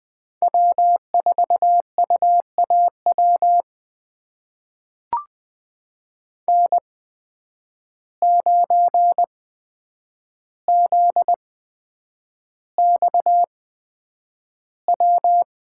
Morse code, heard as a keyed tone, about 20 wpm, spaced as for 9 wpm, 700 Hz. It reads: W4UAW N9ZXW
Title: USA Call Signs - 5 Characters - 8x Character Spacing - 20wpm